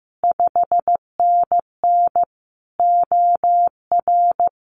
Morse code, heard as a keyed tone, about 15 wpm, 700 Hz. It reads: 5NN OR
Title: Common QSO Phrases Encore 15wpm